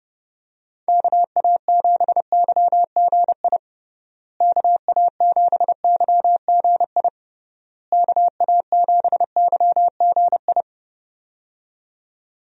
Morse code, heard as a keyed tone, about 30 wpm, 700 Hz. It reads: KA7YGI KA7YGI KA7YGI
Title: USA Call Signs - 6 Characters - Repeated 3x - 30wpm